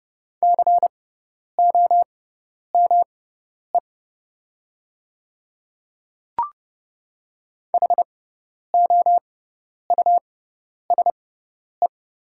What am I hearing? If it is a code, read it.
Morse code, heard as a keyed tone, about 29 wpm, spaced as for 10 wpm, 700 Hz. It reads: COME HOUSE
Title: 6x Character Spacing - Top 200 Words - 30wpm